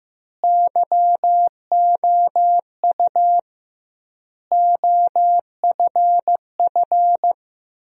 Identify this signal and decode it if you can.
Morse code, heard as a keyed tone, about 16 wpm, 700 Hz. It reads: YOU OFF
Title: 2x Word-Spacing - Sentences from Top 200 Words - 15wpm